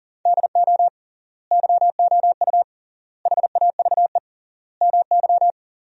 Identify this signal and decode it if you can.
Morse code, heard as a keyed tone, about 40 wpm, 700 Hz. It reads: DO YOU HAVE MY
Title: 3x Word-Spacing - Sentences from Top 500 Words - 40wpm